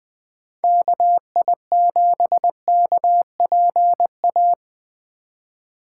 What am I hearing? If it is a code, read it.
Morse code, heard as a keyed tone, about 21 wpm, 700 Hz. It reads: KI7KPA
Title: USA Call Signs - 8x Character Spacing - 20wpm